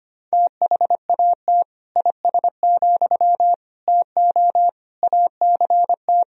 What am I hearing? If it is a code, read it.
Morse code, heard as a keyed tone, about 25 wpm, 700 Hz. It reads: THAT IS, TO ACT
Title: Sentences from Top 1000 Words - Review Part 4 - 25wpm